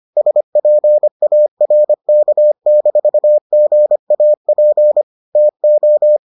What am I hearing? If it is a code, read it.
Morse code, heard as a keyed tone, about 25 wpm, 600 Hz. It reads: SPARK-GAP TO